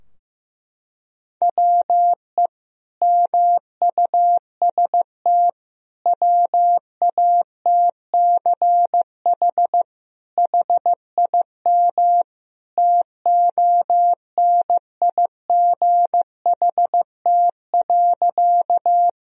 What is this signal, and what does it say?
Morse code, heard as a keyed tone, about 15 wpm, 700 Hz. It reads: WE MUST WATCH HIM TONIGHT.
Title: Sentences from Top 1000 Words - Review Part 3 - 15wpm